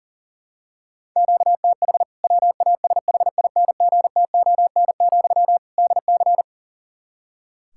Morse code, heard as a keyed tone, about 37 wpm, 700 Hz. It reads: QTH WASHINGTON, DC
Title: K1USN SST QSO Practice 40wpm